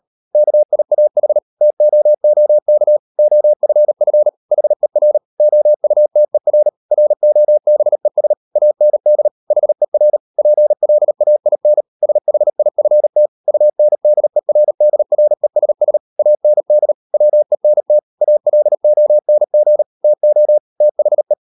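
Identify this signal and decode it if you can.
Morse code, heard as a keyed tone, about 38 wpm, 600 Hz. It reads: KIAH TOOK OFF HER OUTER ROBES AND HER PLAIN SHIFT UNDERDRESS AND WENT ALONG TO THE
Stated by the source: Morse code practice at 38 WPM, sending Ken Follett's novel Never in English, from CW med Gnister.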